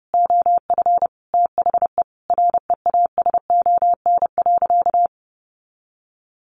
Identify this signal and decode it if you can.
Morse code, heard as a keyed tone, about 30 wpm, 700 Hz. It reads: OF THE REASON.
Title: Sentences from Top 1400 Words 30wpm